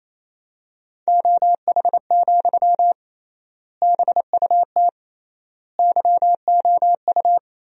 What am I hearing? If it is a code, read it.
Morse code, heard as a keyed tone, about 28 wpm, 700 Hz. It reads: OH, BUT YOU
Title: 3x Word-Spacing - Sentences from Top 500 Words - 28wpm